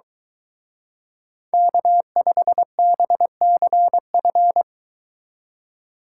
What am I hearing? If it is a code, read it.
Morse code, heard as a keyed tone, about 23 wpm, 700 Hz. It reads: K5BCF